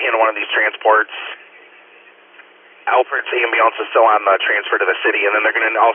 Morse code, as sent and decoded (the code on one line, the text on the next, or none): none